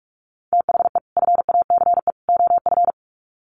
Morse code, heard as a keyed tone, about 44 wpm, 700 Hz. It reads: THE FACE OF
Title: Sets of 3 Words 45wpm